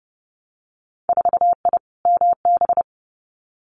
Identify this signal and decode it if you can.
Morse code, heard as a keyed tone, about 30 wpm, 700 Hz. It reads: 4I MB
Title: Winter Field Day Exchanges Encore 30wpm